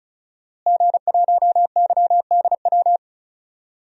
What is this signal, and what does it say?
Morse code, heard as a keyed tone, about 36 wpm, 700 Hz. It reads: G1YDW
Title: DXCC-100 Call Signs - 5 Characters - Repeated 2x - 35wpm